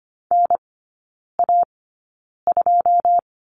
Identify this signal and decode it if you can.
Morse code, heard as a keyed tone, about 25 wpm, spaced as for 9 wpm, 690 Hz. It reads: NA2
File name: Lesson 14e - Number 2 - Three Characters 25wpm